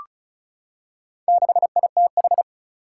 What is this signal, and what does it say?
Morse code, heard as a keyed tone, about 36 wpm, 700 Hz. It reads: 6ITH
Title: Four Challenging Letter-Number Combinations 35wpm